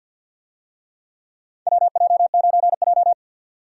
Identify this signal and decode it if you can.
Morse code, heard as a keyed tone, about 51 wpm, 700 Hz. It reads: WJ9J
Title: USA Call Signs - 4 Characters - 50wpm